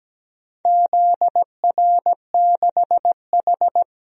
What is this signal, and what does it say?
Morse code, heard as a keyed tone, about 17 wpm, 700 Hz. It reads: ZR6H
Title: DXCC-100 Call Signs - 4 Characters - 17wpm